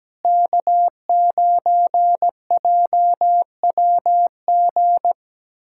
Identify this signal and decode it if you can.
Morse code, heard as a keyed tone, about 17 wpm, 700 Hz. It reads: K9JWG
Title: USA Callsign DE Callsign 17wpm